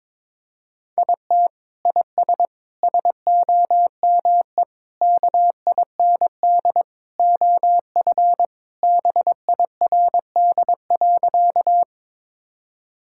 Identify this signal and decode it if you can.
Morse code, heard as a keyed tone, about 22 wpm, 700 Hz. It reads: IT IS SOME KIND OF BIRD.
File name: practice / 22wpm / Sentences with 3-Word Phrases - 22wpm